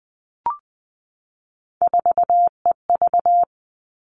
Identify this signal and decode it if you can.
Morse code, heard as a keyed tone, about 20 wpm, 700 Hz. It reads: 4EV
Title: All Cut Numbers - 3-Digit 20wpm